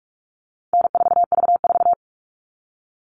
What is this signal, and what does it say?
Morse code, heard as a keyed tone, about 44 wpm, 700 Hz. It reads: N4V4
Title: All Cut Numbers - 4-Digit 45wpm